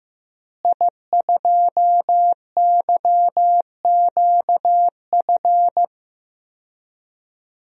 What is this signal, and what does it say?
Morse code, heard as a keyed tone, about 15 wpm, 700 Hz. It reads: I2YQF
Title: DXCC-100 Call Signs - 5 Characters - Repeated 3x - 15wpm